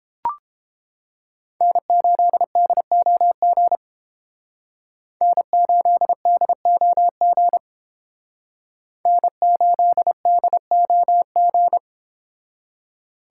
Morse code, heard as a keyed tone, about 28 wpm, 700 Hz. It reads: N8DOG N8DOG N8DOG